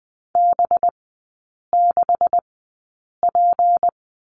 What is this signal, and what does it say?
Morse code, heard as a keyed tone, about 20 wpm, spaced as for 8 wpm, 700 Hz. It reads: B6P
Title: 184 Lesson 30e - Letter b - Three Characters 20wpm